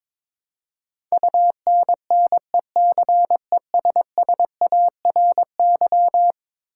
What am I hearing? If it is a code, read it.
Morse code, heard as a keyed tone, about 22 wpm, 700 Hz. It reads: UNNECESSARY
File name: practice / 22wpm / Eleven Letter Words - 22wpm